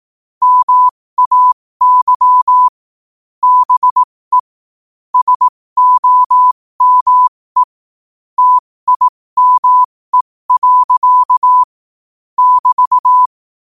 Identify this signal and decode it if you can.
Morse code, heard as a keyed tone, about 18 wpm, spaced as for 15 wpm, 1000 Hz. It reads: MAY BE SOME TIME. <BT>
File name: qotdfiles / 15wpm